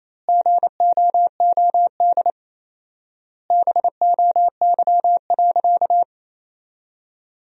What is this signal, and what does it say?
Morse code, heard as a keyed tone, about 28 wpm, 700 Hz. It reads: GOOD BOY.
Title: 4x Word-Spacing - Sentences from Top 200 Words - 28wpm